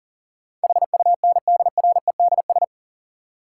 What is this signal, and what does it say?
Morse code, heard as a keyed tone, about 40 wpm, 700 Hz. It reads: HUNDREDS
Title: Sets of 4 Words - New Word Drill - 40wpm